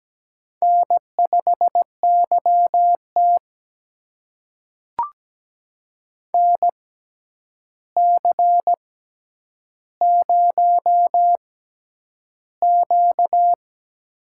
Morse code, heard as a keyed tone, about 17 wpm, spaced as for 9 wpm, 700 Hz. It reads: N5YT NC0Q
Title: USA Call Signs - 4 Characters - 6x Character Spacing - 17wpm